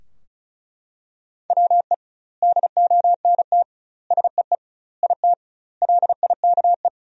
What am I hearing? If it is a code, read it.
Morse code, heard as a keyed tone, about 34 wpm, 700 Hz. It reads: WE DONT SEE IT LIKE
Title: 2x Word-Spacing - Sentences from Top 300 Words - 35wpm